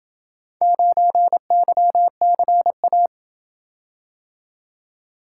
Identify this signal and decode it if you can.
Morse code, heard as a keyed tone, about 27 wpm, 700 Hz. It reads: 9YCA